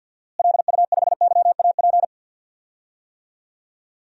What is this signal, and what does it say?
Morse code, heard as a keyed tone, about 46 wpm, 700 Hz. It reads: LU5YAP
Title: DXCC-100 Call Signs - 6 Characters - Repeated 3x - 50wpm